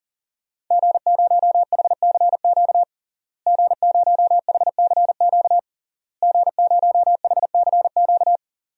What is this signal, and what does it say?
Morse code, heard as a keyed tone, about 40 wpm, 700 Hz. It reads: G0HCQ G0HCQ G0HCQ
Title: DXCC-100 Call Signs - 5 Characters - Repeated 3x - 40wpm